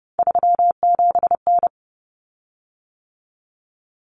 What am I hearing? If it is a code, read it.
Morse code, heard as a keyed tone, about 30 wpm, 700 Hz. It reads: 37N
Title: Common Cut Numbers - 3-Digit 30wpm